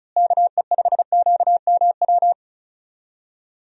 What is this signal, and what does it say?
Morse code, heard as a keyed tone, about 36 wpm, 700 Hz. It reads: KE5QMW
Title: USA Call Signs - 6 Characters - Repeated 3x - 35wpm